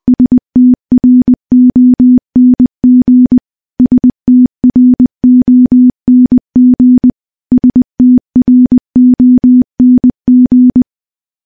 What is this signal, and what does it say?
Morse code, heard as a keyed tone, about 21 wpm, 250 Hz. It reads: STRONG STRONG STRONG
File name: morse-training-20WPM